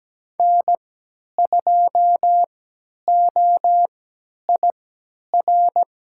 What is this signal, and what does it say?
Morse code, heard as a keyed tone, about 17 wpm, spaced as for 10 wpm, 700 Hz. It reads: N2OIR
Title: USA Call Signs - 3x Character Spacing - 17wpm